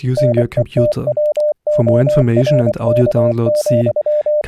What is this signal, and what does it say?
Morse code, heard as a keyed tone, about 24 wpm, 600 Hz, with speech. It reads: NETWORK,
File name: audio notizie / radia s24 n297 radiohelsinki raggam dataradio pt4